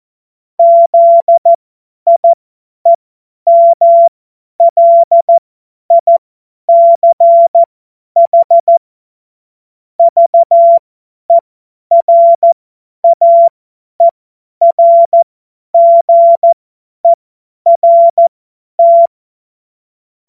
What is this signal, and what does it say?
Morse code, heard as a keyed tone, about 14 wpm, spaced as for 10 wpm, 680 Hz. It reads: ZIEMLICH VERAERGERT
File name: Morsekurs / Witze / Witz 03. Januar 2023 50 F70